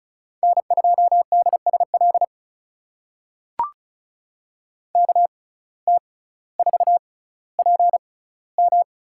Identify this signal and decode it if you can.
Morse code, heard as a keyed tone, about 36 wpm, spaced as for 18 wpm, 700 Hz. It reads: N2DSL KT4PM
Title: USA Call Signs - 5 Characters - 6x Character Spacing - 35wpm